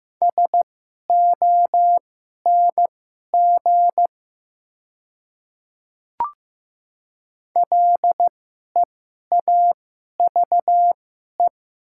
Morse code, heard as a keyed tone, about 15 wpm, spaced as for 11 wpm, 700 Hz. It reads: SONG LEAVE
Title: 2x Character Spacing - Top 500 Words - 15wpm